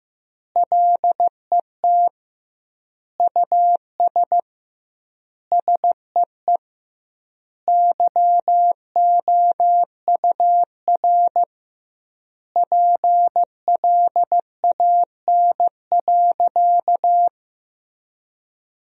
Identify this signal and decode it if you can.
Morse code, heard as a keyed tone, about 15 wpm, 700 Hz. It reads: LET US SEE YOUR PLAN.
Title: 2x Word-Spacing - Sentences from Top 500 Words - 15wpm